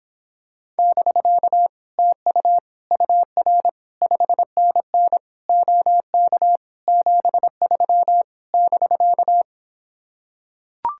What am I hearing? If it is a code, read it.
Morse code, heard as a keyed tone, about 26 wpm, 700 Hz. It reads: <BK> TU UR 5NN OK 73 <BK>